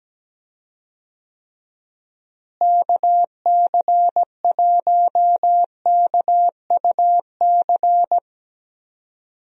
Morse code, heard as a keyed tone, about 17 wpm, 700 Hz. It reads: KC1KUC
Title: USA Call Signs - 6 Characters - Repeated 3x - 17wpm